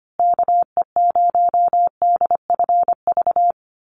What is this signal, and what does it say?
Morse code, heard as a keyed tone, about 25 wpm, 700 Hz. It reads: KE0DFV
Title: USA Call Signs Repeated 3x 25wpm